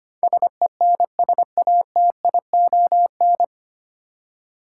Morse code, heard as a keyed tone, about 25 wpm, 700 Hz. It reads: SENSATION
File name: Nine Letter Words - 25wpm